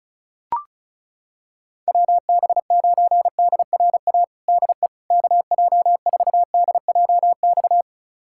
Morse code, heard as a keyed tone, about 36 wpm, 700 Hz. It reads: WB9DRA DE KJ4DJX
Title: USA Callsign DE Callsign - 6 Characters - 35wpm